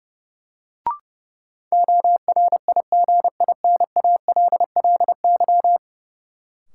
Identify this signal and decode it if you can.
Morse code, heard as a keyed tone, about 32 wpm, 700 Hz. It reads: ORIGINALLY
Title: Sentences from Top 1500 Words - New Word Drill - 30wpm